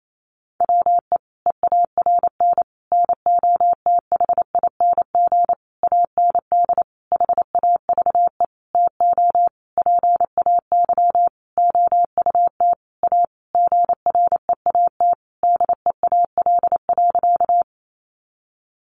Morse code, heard as a keyed tone, about 28 wpm, 700 Hz. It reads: WE EARN NOTHING AND HAVE TO PAY OUT A GREAT DEAL.
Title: Sentences from Top 1500 Words 28wpm